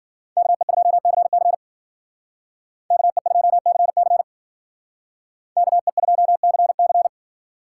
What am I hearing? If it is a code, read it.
Morse code, heard as a keyed tone, about 52 wpm, 700 Hz. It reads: KE2CC KE2CC KE2CC